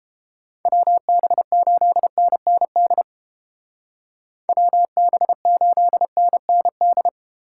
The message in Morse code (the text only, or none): WB8NND WB8NND